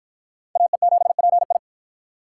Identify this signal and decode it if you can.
Morse code, heard as a keyed tone, about 48 wpm, 700 Hz. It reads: AE7PI